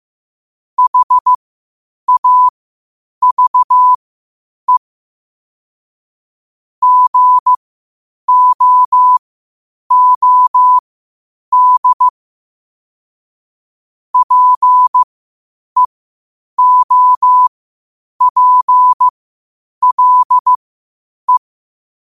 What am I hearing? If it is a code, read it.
Morse code, heard as a keyed tone, about 15 wpm, spaced as for 8 wpm, 1000 Hz. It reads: HAVE GOOD PEOPLE
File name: qotdfiles / 8wpm